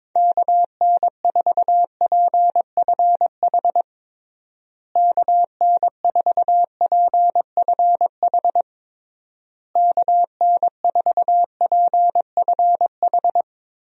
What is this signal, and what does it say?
Morse code, heard as a keyed tone, about 22 wpm, 700 Hz. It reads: KN4PFH KN4PFH KN4PFH
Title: USA Call Signs - 6 Characters - Repeated 3x - 22wpm